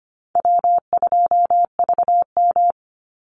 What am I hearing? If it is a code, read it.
Morse code, heard as a keyed tone, about 25 wpm, 700 Hz. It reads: W2VM
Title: Call Signs - 25wpm